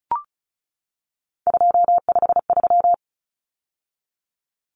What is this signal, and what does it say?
Morse code, heard as a keyed tone, about 34 wpm, 700 Hz. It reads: 253
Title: Numbers - 3-Digit 35wpm